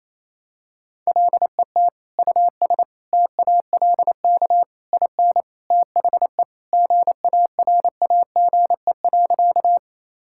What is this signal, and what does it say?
Morse code, heard as a keyed tone, about 28 wpm, 700 Hz. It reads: LET US TALK IN THE GARAGE.
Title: Sentences from Top 1600 Words 28wpm